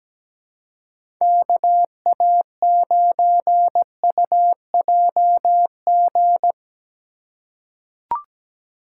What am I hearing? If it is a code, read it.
Morse code, heard as a keyed tone, about 17 wpm, 700 Hz. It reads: KA9UJG
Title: USA Call Signs - 6 Characters - 3x Character Spacing - 17wpm